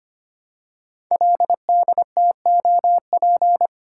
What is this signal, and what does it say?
Morse code, heard as a keyed tone, about 26 wpm, 700 Hz. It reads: LDTOP
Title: Code Groups - 5 Letters - 25wpm